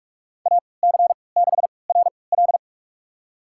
Morse code, heard as a keyed tone, about 41 wpm, spaced as for 25 wpm, 700 Hz. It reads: AC6RL
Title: USA Call Signs - 5 Characters - 3x Character Spacing - 45wpm